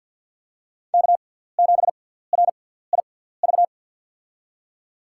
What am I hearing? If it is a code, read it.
Morse code, heard as a keyed tone, about 47 wpm, spaced as for 17 wpm, 700 Hz. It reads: K7RIV
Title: USA Call Signs - 5 Characters - 6x Character Spacing - 50wpm